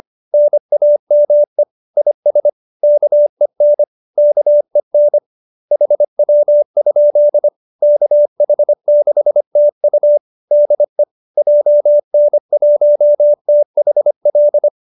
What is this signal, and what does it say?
Morse code, heard as a keyed tone, about 25 wpm, 590 Hz. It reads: NAME IS KEN KEN HW? KH6TU DE JN1THL